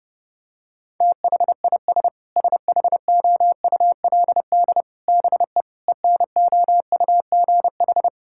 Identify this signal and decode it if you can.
Morse code, heard as a keyed tone, about 32 wpm, 700 Hz. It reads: THIS SHOULD BE ENOUGH
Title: Sentences from Top 1000 Words - Review Part 6 - 30wpm